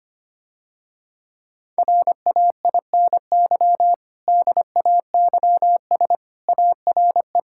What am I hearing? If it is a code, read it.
Morse code, heard as a keyed tone, about 25 wpm, 700 Hz. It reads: RAINY DAYS ARE
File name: practice / 25wpm / Difficult Sentences with Hints - 25wpm